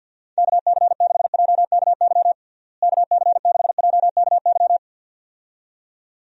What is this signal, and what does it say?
Morse code, heard as a keyed tone, about 51 wpm, 700 Hz. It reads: KC6JKY KC6JKY
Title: USA Call Signs - 6 Characters - Repeated 2x - 50wpm